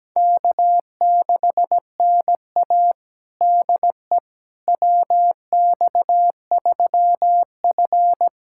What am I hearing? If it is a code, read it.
Morse code, heard as a keyed tone, about 17 wpm, 700 Hz. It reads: K6NA DE WX3F
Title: USA Callsign DE Callsign - 4 Characters - 17wpm